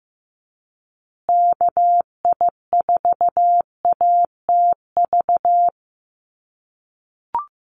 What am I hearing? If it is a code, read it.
Morse code, heard as a keyed tone, about 15 wpm, 700 Hz. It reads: KI4ATV
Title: USA Call Signs Repeated 3x 15wpm